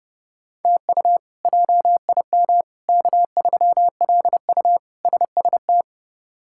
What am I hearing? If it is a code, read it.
Morse code, heard as a keyed tone, about 30 wpm, 700 Hz. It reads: TU JIM K3LU SST
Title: K1USN SST QSO Practice - 30wpm